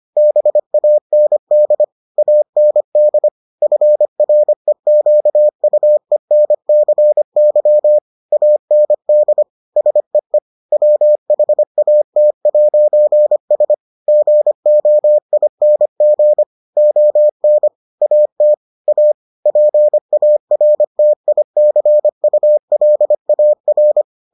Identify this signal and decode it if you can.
Morse code, heard as a keyed tone, about 25 wpm, 600 Hz. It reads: BAND AND FREQUENCY AND SEE WHAT'S GOING ON AT A PARTICULAR